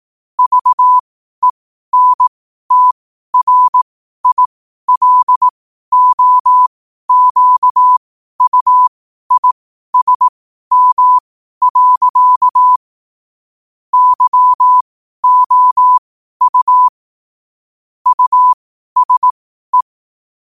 Morse code, heard as a keyed tone, about 18 wpm, spaced as for 12 wpm, 1000 Hz. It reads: VENTRILOQUISM. YOU USE
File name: qotdfiles / 12wpm